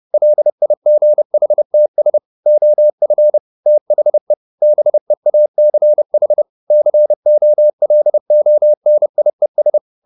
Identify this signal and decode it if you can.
Morse code, heard as a keyed tone, about 30 wpm, 600 Hz. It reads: LIGHTS OF THE BEACH COLONIES